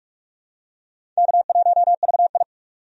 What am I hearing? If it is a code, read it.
Morse code, heard as a keyed tone, about 42 wpm, 700 Hz. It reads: K1VI
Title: USA Call Signs - 4 Characters - 45wpm